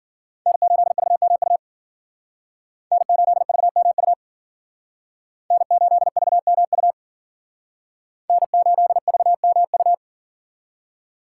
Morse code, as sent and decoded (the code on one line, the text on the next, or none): -. ---.. ...- -- ..- / -. ---.. ...- -- ..- / -. ---.. ...- -- ..- / -. ---.. ...- -- ..-
N8VMU N8VMU N8VMU N8VMU